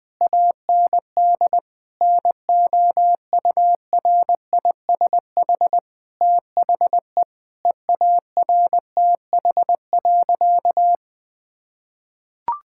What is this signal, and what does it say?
Morse code, heard as a keyed tone, about 20 wpm, 700 Hz. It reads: AND NOURISH THE EARTH.
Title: Difficult Sentences with Hints - 20wpm